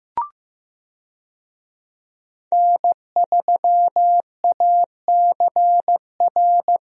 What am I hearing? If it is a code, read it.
Morse code, heard as a keyed tone, about 15 wpm, 700 Hz. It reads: N3ACR
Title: USA Call Signs - 5 Characters - Repeated 3x - 15wpm